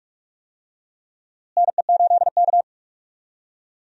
Morse code, heard as a keyed tone, about 46 wpm, 700 Hz. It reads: NE8K
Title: USA Call Signs - 4 Characters - Repeated 3x - 45wpm